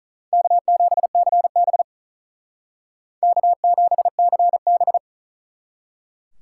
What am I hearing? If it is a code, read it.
Morse code, heard as a keyed tone, about 39 wpm, 700 Hz. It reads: K7CB K7CB
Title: Speed Racing - USA 1x2 Call Signs 35wpm